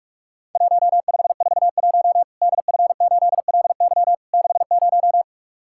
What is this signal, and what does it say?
Morse code, heard as a keyed tone, about 44 wpm, 700 Hz. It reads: 1541 DF8LY 60
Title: Worked All Europe DX Contest - QTC Exchange - 45wpm